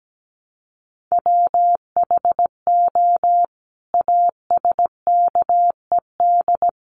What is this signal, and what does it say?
Morse code, heard as a keyed tone, about 17 wpm, 700 Hz. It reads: WHO ASKED
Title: Sets of 5 Words Encore2 17wpm